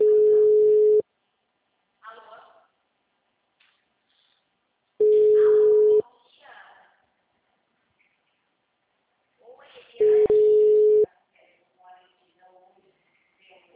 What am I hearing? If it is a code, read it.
Ringback tone (Europe, 425 Hz), with speech.
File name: out- -2011- - - .14034